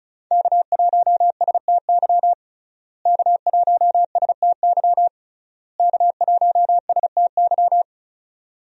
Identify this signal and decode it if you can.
Morse code, heard as a keyed tone, about 34 wpm, 700 Hz. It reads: K1STY K1STY K1STY
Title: USA Call Signs - 5 Characters - Repeated 3x - 35wpm